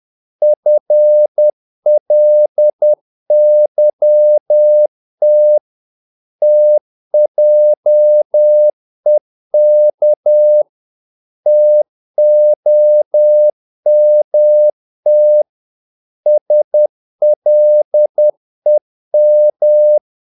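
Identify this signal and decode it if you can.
Morse code, heard as a keyed tone, about 10 wpm, 600 Hz. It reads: FLYT TJEK TOMT SLEM
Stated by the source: Morse practice of short Danish words at 10 wpm, from CW med Gnister.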